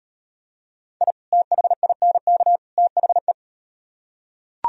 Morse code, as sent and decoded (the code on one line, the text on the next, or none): .. / - .... .. -. -.- / - .... .
I THINK THE